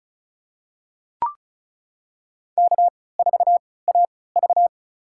Morse code, heard as a keyed tone, about 34 wpm, spaced as for 20 wpm, 700 Hz. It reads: K4AV
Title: USA Call Signs - 4 Characters - 3x Character Spacing - 35wpm